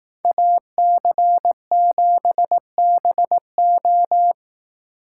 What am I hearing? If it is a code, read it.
Morse code, heard as a keyed tone, about 18 wpm, 700 Hz. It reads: AC7BO
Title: Speed Racing - USA 2x2 Call Signs 15wpm